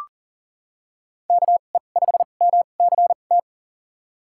Morse code, heard as a keyed tone, about 40 wpm, spaced as for 29 wpm, 700 Hz. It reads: KE5MCT
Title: USA Call Signs - 2x Character Spacing - 40wpm